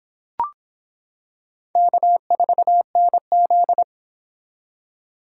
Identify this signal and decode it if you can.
Morse code, heard as a keyed tone, about 25 wpm, 700 Hz. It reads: K4NZ